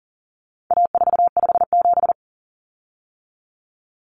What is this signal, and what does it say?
Morse code, heard as a keyed tone, about 40 wpm, 700 Hz. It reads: A457
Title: Common Cut Numbers - 4-Digit 40wpm